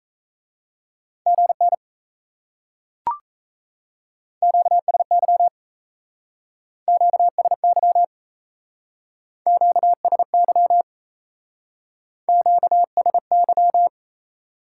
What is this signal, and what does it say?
Morse code, heard as a keyed tone, about 37 wpm, 700 Hz. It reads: GN QSY QSY QSY QSY